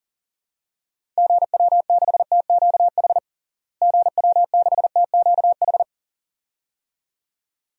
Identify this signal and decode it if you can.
Morse code, heard as a keyed tone, about 37 wpm, 700 Hz. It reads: GW6TQH GW6TQH
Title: DXCC-100 Call Signs - 6 Characters - Repeated 2x - 40wpm